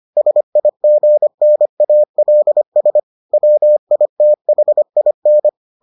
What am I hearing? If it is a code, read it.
Morse code, heard as a keyed tone, about 25 wpm, 600 Hz. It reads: SIGNALS WITHIN